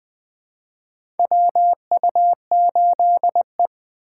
Morse code, heard as a keyed tone, about 20 wpm, 700 Hz. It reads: WU8E
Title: 2023 POTA Hunter Call Signs - 20wpm